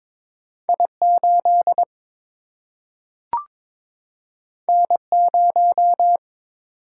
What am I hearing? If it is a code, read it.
Morse code, heard as a keyed tone, about 23 wpm, 700 Hz. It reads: I 8 N 0
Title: Single-Letter Single-Number 22wpm